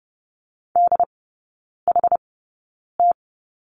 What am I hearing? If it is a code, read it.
Morse code, heard as a keyed tone, about 30 wpm, spaced as for 9 wpm, 700 Hz. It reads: DHT
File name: Lesson 16e - Letter c - Three Characters 30wpm